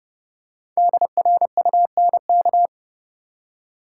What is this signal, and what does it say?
Morse code, heard as a keyed tone, about 30 wpm, 700 Hz. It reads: DRUNK
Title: Five Letter Words - 30wpm